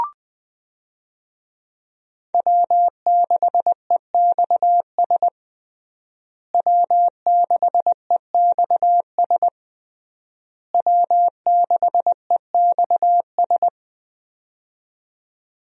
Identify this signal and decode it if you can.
Morse code, heard as a keyed tone, about 20 wpm, 700 Hz. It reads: W6EXS W6EXS W6EXS
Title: USA Call Signs - 5 Characters - Repeated 3x - 20wpm